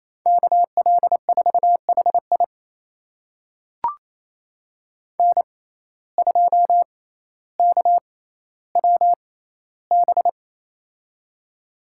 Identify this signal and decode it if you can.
Morse code, heard as a keyed tone, about 28 wpm, spaced as for 14 wpm, 700 Hz. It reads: KL4HI N2KWB
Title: USA Call Signs - 6x Character Spacing - 28wpm